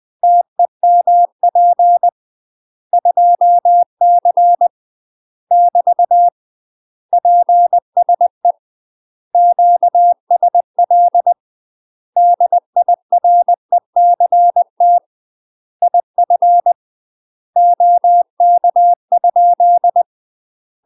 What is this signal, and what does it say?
Morse code, heard as a keyed tone, about 20 wpm, 700 Hz. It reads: TEMP 2C <BT> PSE QSL DIRECT IF OK?
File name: UK MASTER QSOs TRAINING/20WPM